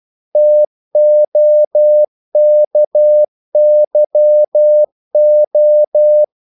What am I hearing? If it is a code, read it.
Morse code, heard as a keyed tone, about 12 wpm, 600 Hz. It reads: TOKYO
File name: CW / Practice / NationalCapitals12